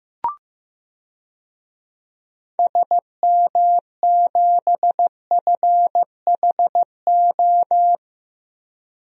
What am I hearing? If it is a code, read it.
Morse code, heard as a keyed tone, about 15 wpm, 700 Hz. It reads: SM7FHO
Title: DXCC-100 Call Signs - 6 Characters - Repeated 3x - 15wpm